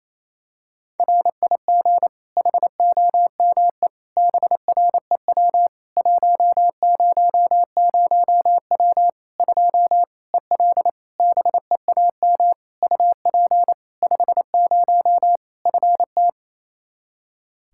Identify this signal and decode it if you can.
Morse code, heard as a keyed tone, about 27 wpm, 700 Hz. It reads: RIG HOME BREW 100W 2 EL BEAM UP 50 FT
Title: Practice QSO - Exchange 2 - Info - 28wpm